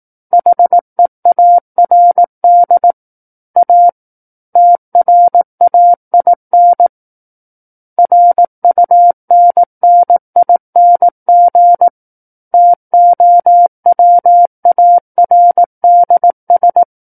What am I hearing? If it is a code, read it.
Morse code, heard as a keyed tone, about 18 wpm, 700 Hz. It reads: HEARD A TRAIN RUNNING TOWARDS
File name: 18 WPM